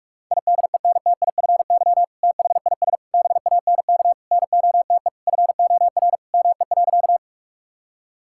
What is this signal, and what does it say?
Morse code, heard as a keyed tone, about 46 wpm, 700 Hz. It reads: IDENTIFY THIS BANK NOTE FOR ME.
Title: Sentences from Top 1000 Words - Review Part 6 - 45wpm